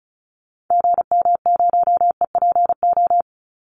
Morse code, heard as a keyed tone, about 34 wpm, 700 Hz. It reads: GM0EPO
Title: Call Signs Repeated 2x - 35wpm